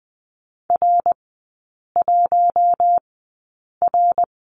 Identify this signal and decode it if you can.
Morse code, heard as a keyed tone, about 20 wpm, spaced as for 8 wpm, 700 Hz. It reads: R1R
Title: 053 Lesson 10e - Letter r - Three Characters 20wpm